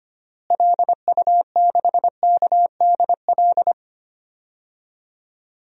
Morse code, heard as a keyed tone, about 25 wpm, 700 Hz. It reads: LU6KDL
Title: DXCC-100 Call Signs - 6 Characters - Repeated 3x - 25wpm